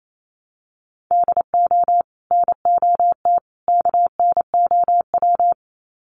Morse code, heard as a keyed tone, about 28 wpm, 700 Hz. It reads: DO NOT KNOW
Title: Sets of 3 Words 28wpm